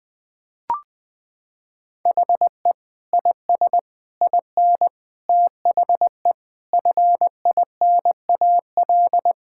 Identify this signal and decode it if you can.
Morse code, heard as a keyed tone, about 20 wpm, 700 Hz. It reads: HE IS IN THE FINAL
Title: Sentences with 8-Word Phrases - 20wpm